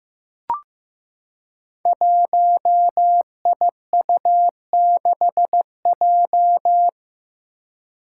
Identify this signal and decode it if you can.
Morse code, heard as a keyed tone, about 15 wpm, 700 Hz. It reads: 1IU6J
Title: Five Challenging Letter-Number Combinations 15wpm